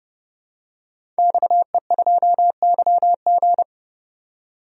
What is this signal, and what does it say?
Morse code, heard as a keyed tone, about 30 wpm, 700 Hz. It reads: XE2YG
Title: DXCC-100 Call Signs - 5 Characters - 30wpm